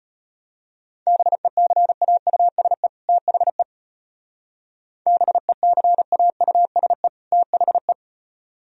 Morse code, heard as a keyed tone, about 32 wpm, 700 Hz. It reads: BECAUSE THE BECAUSE THE